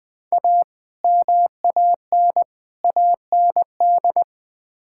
Morse code, heard as a keyed tone, about 20 wpm, 700 Hz. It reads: A MAN AND
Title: Build A Phrase - 5-Word Phrases - 20wpm